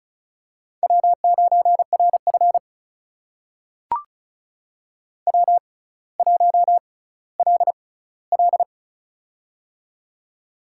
Morse code, heard as a keyed tone, about 36 wpm, spaced as for 18 wpm, 700 Hz. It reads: W9RF W1LL
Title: USA Call Signs - 4 Characters - 6x Character Spacing - 35wpm